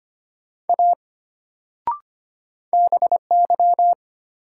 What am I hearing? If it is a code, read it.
Morse code, heard as a keyed tone, about 25 wpm, 700 Hz. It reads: A BY
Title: Sentences from Top 100 Words - New Word Drill - 25wpm